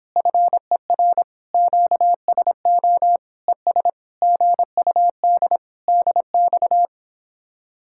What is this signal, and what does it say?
Morse code, heard as a keyed tone, about 26 wpm, 700 Hz. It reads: FER QSO ES GUD DX